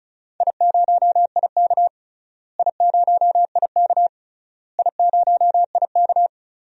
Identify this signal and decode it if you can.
Morse code, heard as a keyed tone, about 34 wpm, 700 Hz. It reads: I0IK I0IK I0IK
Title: DXCC-100 Call Signs - 4 Characters - Repeated 3x - 35wpm